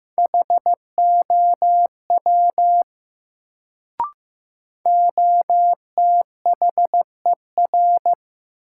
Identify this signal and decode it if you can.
Morse code, heard as a keyed tone, about 15 wpm, 700 Hz. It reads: HOW OTHER
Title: Sentences from Top 100 Words - New Word Drill - 15wpm